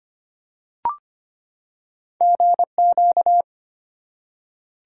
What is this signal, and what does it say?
Morse code, heard as a keyed tone, about 25 wpm, 700 Hz. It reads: GQ
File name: Two Letter-Number Combinations - 25wpm